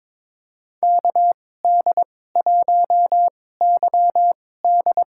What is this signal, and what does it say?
Morse code, heard as a keyed tone, about 22 wpm, spaced as for 16 wpm, 700 Hz. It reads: KD1YD
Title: USA Call Signs - 2x Character Spacing - 22wpm